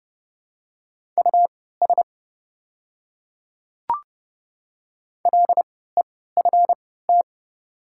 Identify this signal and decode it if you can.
Morse code, heard as a keyed tone, about 30 wpm, spaced as for 17 wpm, 700 Hz. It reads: US LEFT
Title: 3x Character Spacing - Top 500 Words - 30wpm